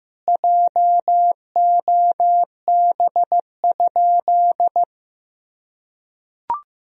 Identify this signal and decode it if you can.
Morse code, heard as a keyed tone, about 15 wpm, 700 Hz. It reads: JOB?
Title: Sentences from Top 1000 Words - Review Part 3 - 15wpm